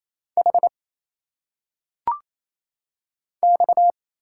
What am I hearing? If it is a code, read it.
Morse code, heard as a keyed tone, about 28 wpm, 700 Hz. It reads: H X
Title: Single Letters - Phonetic Pronunciation 28wpm